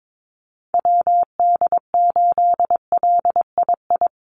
Morse code, heard as a keyed tone, about 22 wpm, 700 Hz. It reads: WD8LII
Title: USA Call Signs Repeated 2x 22wpm